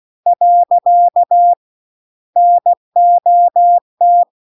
Morse code, heard as a keyed tone, about 16 wpm, 700 Hz. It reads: . NOT